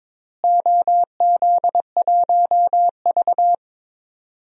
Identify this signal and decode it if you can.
Morse code, heard as a keyed tone, about 22 wpm, 700 Hz. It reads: OZ1V
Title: DXCC-100 Call Signs - 4 Characters - Repeated 2x - 22wpm